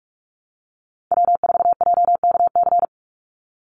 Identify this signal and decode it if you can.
Morse code, heard as a keyed tone, about 44 wpm, 700 Hz. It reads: W4JKC
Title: NA SOTA Chaser Callsigns 45wpm